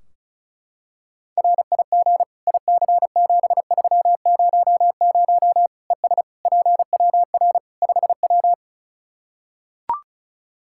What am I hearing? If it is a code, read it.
Morse code, heard as a keyed tone, about 34 wpm, 700 Hz. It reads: RIG IC7300 ES PWR 5W
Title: Practice QSO - Exchange 2 - Info - 35wpm